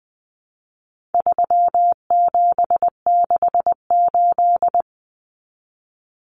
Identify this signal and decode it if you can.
Morse code, heard as a keyed tone, about 20 wpm, 700 Hz. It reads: 3768
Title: Numbers - 4-Digit 20wpm